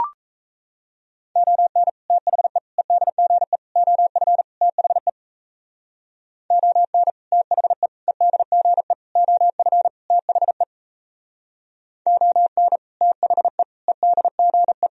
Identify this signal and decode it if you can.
Morse code, heard as a keyed tone, about 40 wpm, 700 Hz. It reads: ON THE EDGE OF THE ON THE EDGE OF THE ON THE EDGE